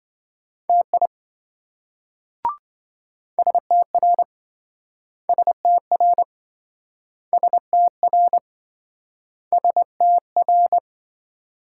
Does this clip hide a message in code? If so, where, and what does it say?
Yes: Morse code from 0.7 to 10.8 s, heard as a keyed tone, about 27 wpm, 700 Hz. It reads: TI STR STR STR STR